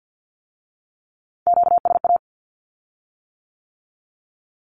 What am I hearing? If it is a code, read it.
Morse code, heard as a keyed tone, about 50 wpm, 700 Hz. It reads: QSA
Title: Q-Codes with Spoken Meaning - 50wpm